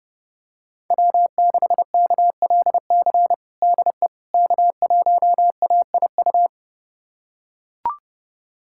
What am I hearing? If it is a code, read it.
Morse code, heard as a keyed tone, about 32 wpm, 700 Hz. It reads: W6KLC DE K1AIU
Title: USA Callsign DE Callsign - 5 Characters - 30wpm